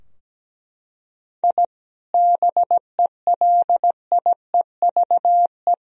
Morse code, heard as a keyed tone, about 17 wpm, 700 Hz. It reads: I BELIEVE
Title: Sentences from Top 1000 Words - Review Part 3 - 17wpm